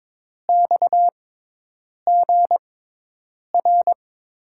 Morse code, heard as a keyed tone, about 22 wpm, spaced as for 8 wpm, 700 Hz. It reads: XGR
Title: USA Call Signs - 5 Characters - 6x Character Spacing - 22wpm